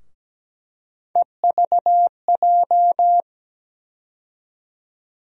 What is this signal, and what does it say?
Morse code, heard as a keyed tone, about 17 wpm, 700 Hz. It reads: EVJ
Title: USA Call Sign Postfixes - 3 Letters 17wpm